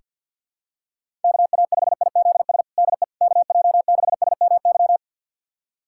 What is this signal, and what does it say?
Morse code, heard as a keyed tone, about 46 wpm, 700 Hz. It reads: KA5IZS DE KJ6SMY
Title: USA Callsign DE Callsign - 6 Characters - 50wpm